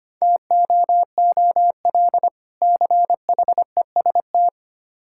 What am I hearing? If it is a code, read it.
Morse code, heard as a keyed tone, about 25 wpm, 690 Hz. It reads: TOOL CHEST
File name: practice / 25wpm / Difficult Words - Speed-Racing with Hints - 25wpm